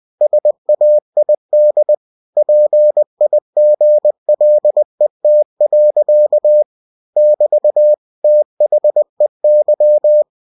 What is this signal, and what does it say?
Morse code, heard as a keyed tone, about 20 wpm, 600 Hz. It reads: SAID PIGLET. <BT> THEY
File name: cw-books / winnie-the-pooh / winnie-20wpm-page-0036